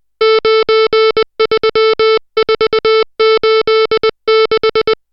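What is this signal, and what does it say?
Morse code, heard as a keyed tone, about 20 wpm, 440 Hz, with noise. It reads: 93486
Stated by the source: Morse code drill of digits at 60 characters per minute.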